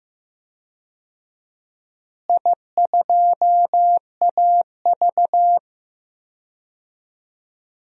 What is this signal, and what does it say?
Morse code, heard as a keyed tone, about 15 wpm, 700 Hz. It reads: I2AV
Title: DXCC-100 Call Signs - 4 Characters - Repeated 2x - 15wpm